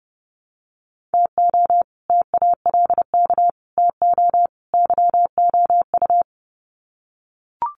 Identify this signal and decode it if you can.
Morse code, heard as a keyed tone, about 28 wpm, 700 Hz. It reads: TO TALK TO YOU
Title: Sets of 4 Words 30wpm